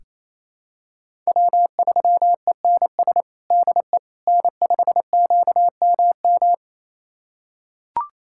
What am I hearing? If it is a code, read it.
Morse code, heard as a keyed tone, about 28 wpm, 700 Hz. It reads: W3ENS DE N5QMM
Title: USA Callsign DE Callsign - 5 Characters - 28wpm